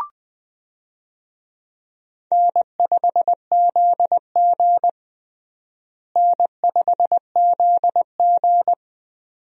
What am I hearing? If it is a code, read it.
Morse code, heard as a keyed tone, about 20 wpm, 700 Hz. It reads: N5ZG N5ZG
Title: USA Call Signs - 4 Characters - Repeated 2x - 20wpm